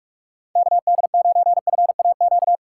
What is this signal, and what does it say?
Morse code, heard as a keyed tone, about 59 wpm, 700 Hz. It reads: KD9FAQ